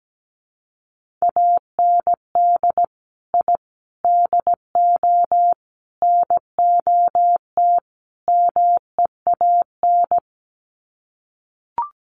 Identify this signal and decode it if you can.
Morse code, heard as a keyed tone, about 17 wpm, 700 Hz. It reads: AND I DO NOT MEAN
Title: Sets of 5 Words Encore2 17wpm